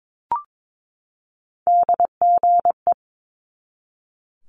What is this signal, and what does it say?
Morse code, heard as a keyed tone, about 22 wpm, 700 Hz. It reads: DGE
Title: Multi-letter Phonemes 22wpm